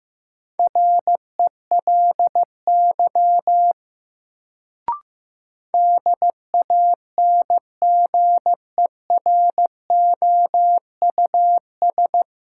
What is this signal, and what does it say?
Morse code, heard as a keyed tone, about 15 wpm, 700 Hz. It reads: RELY DANGEROUS
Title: Sentences from Top 1200 Words - New Word Drill - 15wpm